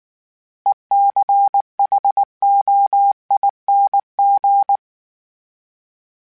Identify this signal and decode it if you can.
Morse code, heard as a keyed tone, about 19 wpm, 800 Hz. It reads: ECHOING
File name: Special INGs / Special INGs-mp3s / Sngl Ing @18